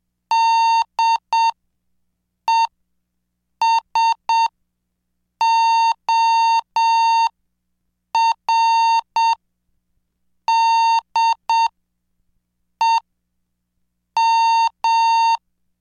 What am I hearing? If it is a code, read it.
Morse code, heard as a keyed tone, about 7 wpm, spaced as for 5 wpm, 910 Hz. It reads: DESORDEM
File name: cursos / CDestudo / 26 Track 26